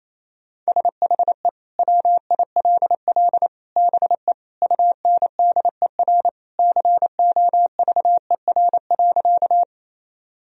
Morse code, heard as a keyed tone, about 27 wpm, 700 Hz. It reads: SHE WILL BE UNDER COVER.
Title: Sentences from Top 1000 Words - Review Part 1 - 28wpm